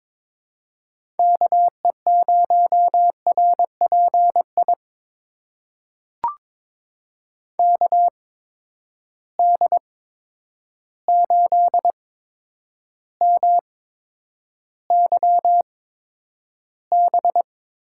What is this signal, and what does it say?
Morse code, heard as a keyed tone, about 22 wpm, spaced as for 9 wpm, 700 Hz. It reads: KE0RPI KD8MYB
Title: USA Call Signs - 8x Character Spacing - 22wpm